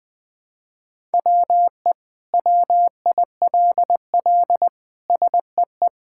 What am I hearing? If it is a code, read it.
Morse code, heard as a keyed tone, about 21 wpm, 700 Hz. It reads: WE WILL SEE
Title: Sentences with 7-Word Phrases - Spoken Beginning - 20wpm